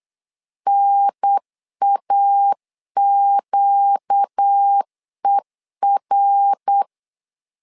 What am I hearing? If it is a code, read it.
Morse code, heard as a keyed tone, about 8 wpm, 800 Hz. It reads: NAQER
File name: LEZIONI / 4 Q1S 8wpm 2710